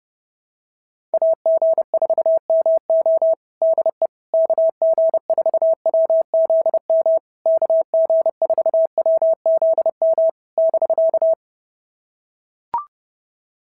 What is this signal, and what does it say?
Morse code, heard as a keyed tone, about 30 wpm, 650 Hz. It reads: AG4MO DE KG4WZM KG4WZM <BK>
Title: SOTA QSO Practice 30wpm